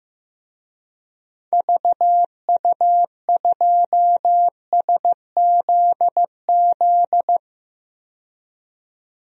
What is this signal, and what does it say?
Morse code, heard as a keyed tone, about 15 wpm, 700 Hz. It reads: VU2SZZ
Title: DXCC-100 Call Signs - 6 Characters - 15wpm